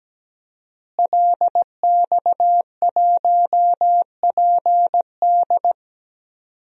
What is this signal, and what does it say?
Morse code, heard as a keyed tone, about 17 wpm, 700 Hz. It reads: LX1PD
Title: DXCC-100 Call Signs - 5 Characters - Repeated 3x - 17wpm